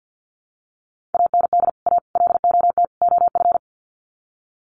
Morse code, heard as a keyed tone, about 50 wpm, 700 Hz. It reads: AND A LOT OF
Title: Sets of 4 Words 50wpm